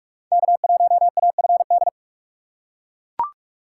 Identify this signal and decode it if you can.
Morse code, heard as a keyed tone, about 42 wpm, 700 Hz. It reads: K1AFD
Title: USA Call Signs - 4x Character Spacing - 45wpm